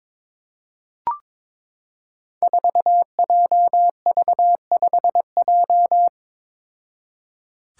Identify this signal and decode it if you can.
Morse code, heard as a keyed tone, about 22 wpm, 700 Hz. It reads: 4JV5J
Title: Five Challenging Letter-Number Combinations 22wpm